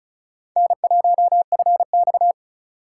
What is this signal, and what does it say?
Morse code, heard as a keyed tone, about 36 wpm, 690 Hz. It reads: N1FX
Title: USA Call Signs - 4 Characters - 2x Character Spacing - 35wpm